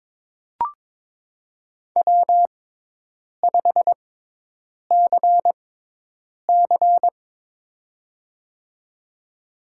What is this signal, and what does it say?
Morse code, heard as a keyed tone, about 22 wpm, spaced as for 8 wpm, 700 Hz. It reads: W5CC
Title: USA Call Signs - 4 Characters - 6x Character Spacing - 22wpm